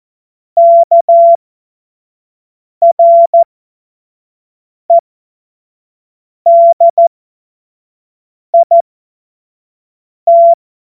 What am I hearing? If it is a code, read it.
Morse code, heard as a keyed tone, about 14 wpm, spaced as for 5 wpm, 680 Hz. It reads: KREDIT
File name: Morsekurs / Witze / Witz 02. Mai 2023 25 F70